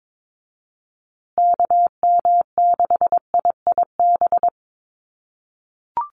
Morse code, heard as a keyed tone, about 22 wpm, 700 Hz. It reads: KM6IIB
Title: USA Call Signs Repeated 2x 22wpm